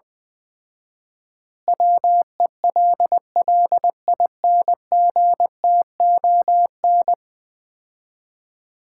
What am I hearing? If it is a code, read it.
Morse code, heard as a keyed tone, about 20 wpm, 700 Hz. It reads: WELLINGTON
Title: Common QSO Phrases - 4x Word-Spacing - 20wpm